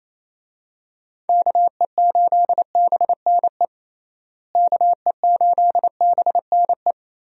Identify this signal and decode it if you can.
Morse code, heard as a keyed tone, about 28 wpm, 700 Hz. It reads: KE8BNE KE8BNE
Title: USA Call Signs - 6 Characters - Repeated 2x - 28wpm